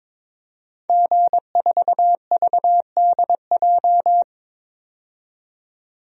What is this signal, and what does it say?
Morse code, heard as a keyed tone, about 22 wpm, 700 Hz. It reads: G4VDJ
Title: DXCC-100 Call Signs - 5 Characters - 22wpm